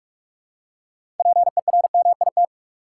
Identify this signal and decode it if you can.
Morse code, heard as a keyed tone, about 46 wpm, 700 Hz. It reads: PERMIT
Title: Six Letter Words - 45wpm